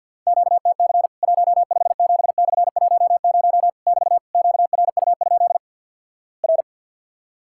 Morse code, heard as a keyed tone, about 50 wpm, 700 Hz. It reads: QTC 157/10 <BT> QRU? R
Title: Worked All Europe DX Contest - QTC Exchange - 50wpm